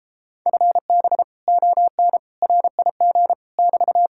Morse code, heard as a keyed tone, about 33 wpm, 700 Hz. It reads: FB ON RIG <BT>